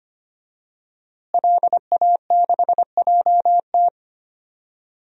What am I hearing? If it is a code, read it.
Morse code, heard as a keyed tone, about 25 wpm, 700 Hz. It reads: LA6JT
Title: DXCC-100 Call Signs - 5 Characters - Repeated 3x - 25wpm